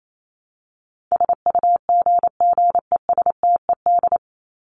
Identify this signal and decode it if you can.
Morse code, heard as a keyed tone, about 28 wpm, 700 Hz. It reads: SUGGESTED
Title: ED Suffix - Top 100 Words with Base Words 28wpm